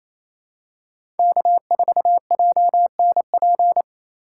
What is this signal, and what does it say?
Morse code, heard as a keyed tone, about 28 wpm, 700 Hz. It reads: K4JNP
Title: USA Call Signs - 3x Character Spacing - 28wpm